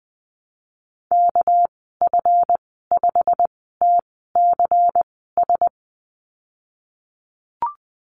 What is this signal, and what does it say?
Morse code, heard as a keyed tone, about 20 wpm, spaced as for 15 wpm, 700 Hz. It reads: KF5TCS
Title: 200 Lesson 32g - Letter k - Call Signs 20wpm